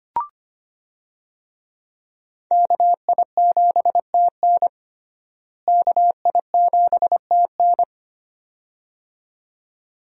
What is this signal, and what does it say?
Morse code, heard as a keyed tone, about 25 wpm, 700 Hz. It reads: KI7TN KI7TN
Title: USA Call Signs - 5 Characters - Repeated 2x - 25wpm